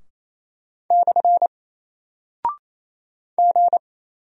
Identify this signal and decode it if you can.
Morse code, heard as a keyed tone, about 27 wpm, 700 Hz. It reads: / G
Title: Single Character (Letter:Number:Punctuation:Prosign) Alt 28wpm